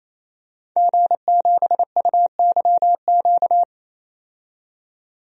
Morse code, heard as a keyed tone, about 28 wpm, 700 Hz. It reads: G7UYQ
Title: DXCC-100 Call Signs - 5 Characters - 28wpm